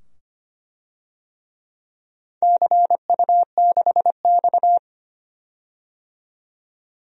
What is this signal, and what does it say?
Morse code, heard as a keyed tone, about 25 wpm, 700 Hz. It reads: CU6X
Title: DXCC-100 Call Signs - 4 Characters - 25wpm